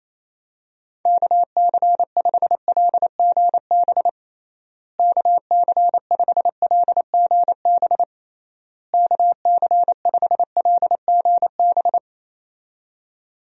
Morse code, heard as a keyed tone, about 28 wpm, 700 Hz. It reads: KC5LGB KC5LGB KC5LGB
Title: USA Call Signs - 6 Characters - Repeated 3x - 28wpm